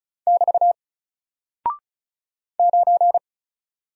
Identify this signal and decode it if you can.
Morse code, heard as a keyed tone, about 36 wpm, 700 Hz. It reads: <BT> 9
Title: Single Character (Letter:Number:Punctuation:Prosign) Alt 35wpm